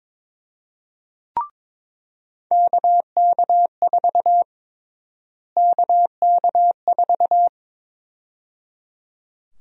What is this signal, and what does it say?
Morse code, heard as a keyed tone, about 22 wpm, 700 Hz. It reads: KK4 KK4
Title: USA Call Sign Prefixes - Repeated 2x - 22wpm